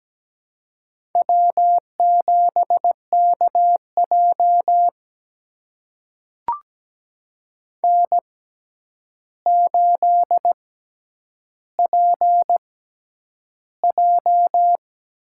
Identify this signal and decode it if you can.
Morse code, heard as a keyed tone, about 17 wpm, spaced as for 9 wpm, 700 Hz. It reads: W7KJ N8PJ
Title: USA Call Signs - 4 Characters - 6x Character Spacing - 17wpm